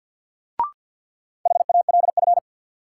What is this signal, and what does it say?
Morse code, heard as a keyed tone, about 53 wpm, 700 Hz. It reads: HALF
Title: Sets of 3 Words - New Word Drill - 50wpm